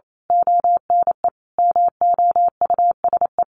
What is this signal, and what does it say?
Morse code, heard as a keyed tone, about 28 wpm, 700 Hz. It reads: ONE MOUSE
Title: A Sound Of Thunder 28wpm